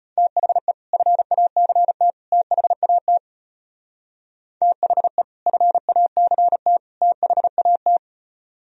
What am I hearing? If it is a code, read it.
Morse code, heard as a keyed tone, about 33 wpm, 700 Hz. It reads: THE FACT THAT THE FACT THAT